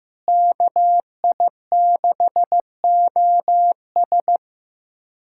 Morse code, heard as a keyed tone, about 15 wpm, 700 Hz. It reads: KI6OS
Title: USA Callsign DE Callsign - 5 Characters - 15wpm